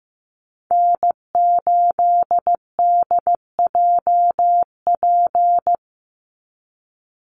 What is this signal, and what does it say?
Morse code, heard as a keyed tone, about 15 wpm, 700 Hz. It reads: N8DJP
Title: USA Call Signs 15wpm